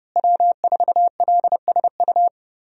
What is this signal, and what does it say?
Morse code, heard as a keyed tone, about 30 wpm, 700 Hz. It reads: W4LSU
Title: USA Call Signs - 5 Characters - Repeated 3x - 30wpm